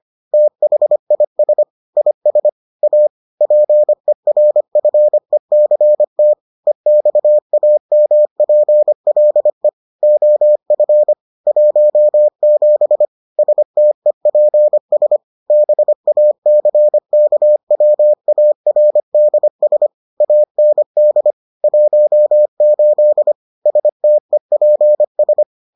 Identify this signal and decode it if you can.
Morse code, heard as a keyed tone, about 25 wpm, 600 Hz. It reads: THIS IS A PERFECT EXAMPLE OF 17 STEPS BACKWARDS AND 18 STEPS